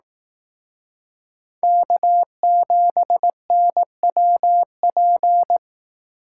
Morse code, heard as a keyed tone, about 18 wpm, 700 Hz. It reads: K7NWP
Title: Speed Racing - USA 1x3 Call Signs 15wpm